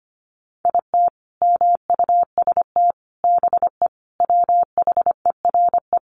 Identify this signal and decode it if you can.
Morse code, heard as a keyed tone, about 25 wpm, 700 Hz. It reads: IT MUST BE WHERE
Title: Sentences from Top 200 Words 25wpm